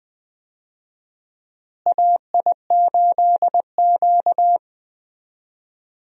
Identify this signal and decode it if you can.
Morse code, heard as a keyed tone, about 20 wpm, 700 Hz. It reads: AI8Q
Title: USA Call Signs - 4 Characters - Repeated 3x - 20wpm